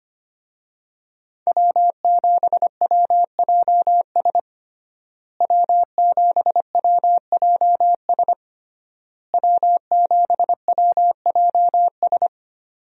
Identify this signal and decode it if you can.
Morse code, heard as a keyed tone, about 25 wpm, 700 Hz. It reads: W7WJS W7WJS W7WJS
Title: USA Call Signs - 5 Characters - Repeated 3x - 25wpm